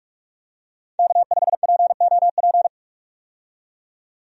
Morse code, heard as a keyed tone, about 46 wpm, 700 Hz. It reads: K5POP
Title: USA Call Signs - 5 Characters - 45wpm